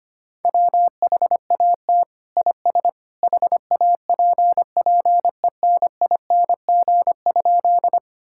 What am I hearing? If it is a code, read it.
Morse code, heard as a keyed tone, about 25 wpm, 700 Hz. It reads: WHAT IS HAPPENING?
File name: practice / 25wpm / Sentences with 2-Word Phrases - 25wpm